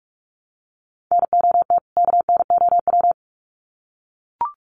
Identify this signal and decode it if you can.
Morse code, heard as a keyed tone, about 44 wpm, 700 Hz. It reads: NOT KNOW
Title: Sets of 2 Words 45wpm